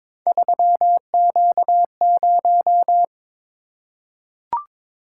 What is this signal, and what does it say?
Morse code, heard as a keyed tone, about 22 wpm, 700 Hz. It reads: 3Q0
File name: practice / 22wpm / Three Letter-Number Combinations - 22wpm